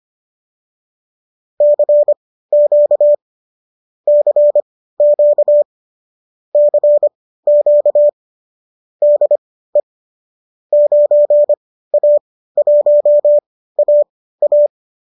Morse code, heard as a keyed tone, about 25 wpm, spaced as for 15 wpm, 590 Hz. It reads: CQ CQ CQ DE 9A1AA